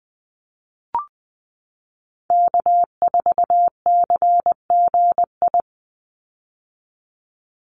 Morse code, heard as a keyed tone, about 20 wpm, 700 Hz. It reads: K4CGI
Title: USA Call Signs 20wpm